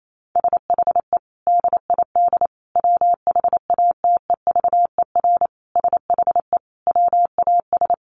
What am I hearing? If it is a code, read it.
Morse code, heard as a keyed tone, about 28 wpm, 700 Hz. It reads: SHE DID WHATEVER SHE WAS
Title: Sentences from Top 1100 Words 28wpm